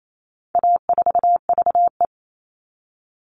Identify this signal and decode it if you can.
Morse code, heard as a keyed tone, about 28 wpm, 700 Hz. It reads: A4VE
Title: All Cut Numbers - 4-Digit 28wpm